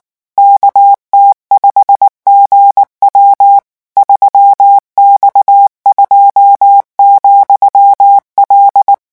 Morse code, heard as a keyed tone, about 19 wpm, 800 Hz. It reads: KT5GW 3X2,L
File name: morse mp3 / revision13-vitesse 18 mots